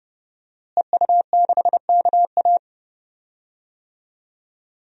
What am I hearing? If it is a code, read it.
Morse code, heard as a keyed tone, about 31 wpm, 700 Hz. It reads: EU6KA
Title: DXCC-100 Call Signs - 5 Characters - Repeated 2x - 30wpm